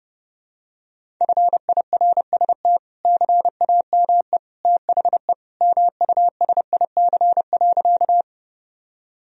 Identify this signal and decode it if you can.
Morse code, heard as a keyed tone, about 30 wpm, 700 Hz. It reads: FIRST CAME THE MUSIC.
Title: Sentences from Top 1000 Words - Review Part 1 - 30wpm